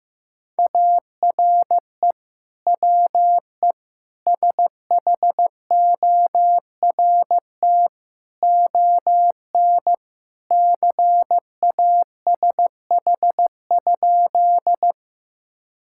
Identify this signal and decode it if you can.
Morse code, heard as a keyed tone, about 16 wpm, 700 Hz. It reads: ARE WE SHORT ON CASH?
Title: Sentences from Top 1000 Words - Review Part 5 - 15wpm